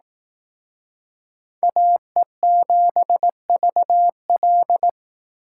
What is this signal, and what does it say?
Morse code, heard as a keyed tone, about 18 wpm, 700 Hz. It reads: AE7VL
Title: Speed Racing - USA 2x2 Call Signs 15wpm